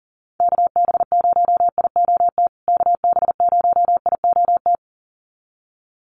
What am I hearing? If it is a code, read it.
Morse code, heard as a keyed tone, about 40 wpm, 700 Hz. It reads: KB0IOT KB0IOT
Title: USA Call Signs Repeated 2x 40wpm